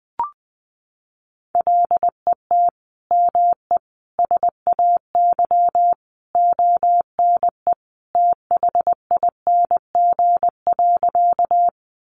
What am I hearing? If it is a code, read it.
Morse code, heard as a keyed tone, about 20 wpm, 700 Hz. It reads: LET ME SAY ONE THING.
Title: Sentences from Top 300 Words 20wpm